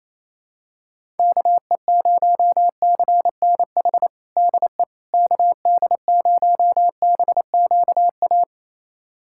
Morse code, heard as a keyed tone, about 28 wpm, 700 Hz. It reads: KE0CNH DE KD0BQA
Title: USA Callsign DE Callsign - 6 Characters - 28wpm